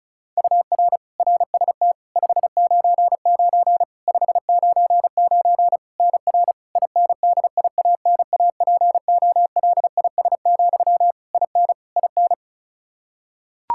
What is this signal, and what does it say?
Morse code, heard as a keyed tone, about 36 wpm, 700 Hz. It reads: UR RST 599 599 NR INDIANAPOLIS, IN IN
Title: Practice QSO - Exchange 1 - Beginning - 35wpm